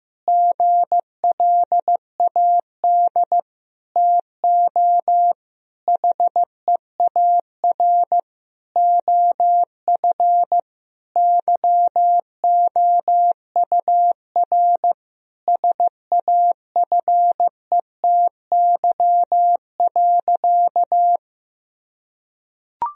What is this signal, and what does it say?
Morse code, heard as a keyed tone, about 15 wpm, 700 Hz. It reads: GLAD TO HEAR OF YOUR SAFETY.
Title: Sentences from Top 1000 Words - Review Part 4 - 15wpm